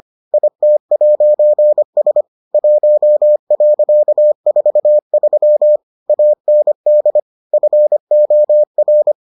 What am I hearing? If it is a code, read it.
Morse code, heard as a keyed tone, about 25 wpm, 600 Hz. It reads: IT'S 1.43 AND FOR